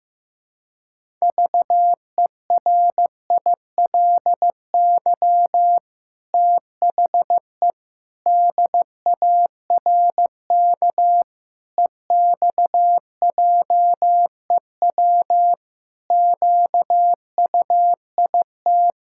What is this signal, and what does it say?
Morse code, heard as a keyed tone, about 15 wpm, 700 Hz. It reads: VERILY THE DARK EXJEW QUIT
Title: Pangram Challenge v2 - 15wpm